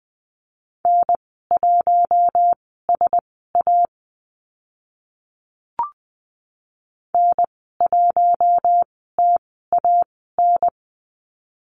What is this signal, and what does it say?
Morse code, heard as a keyed tone, about 20 wpm, spaced as for 15 wpm, 700 Hz. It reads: N1SA N1TAN
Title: 042 Lesson 8g - Number 1 - Call Signs 20wpm